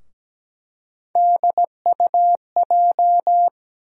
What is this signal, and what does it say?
Morse code, heard as a keyed tone, about 17 wpm, 700 Hz. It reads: DUJ
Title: USA Call Sign Postfixes - 3 Letters 17wpm